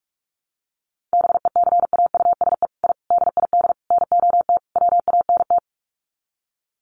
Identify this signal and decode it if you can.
Morse code, heard as a keyed tone, about 44 wpm, 700 Hz. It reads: BECAUSE I DID NOT WANT
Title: Sets of 5 Words Encore2 45wpm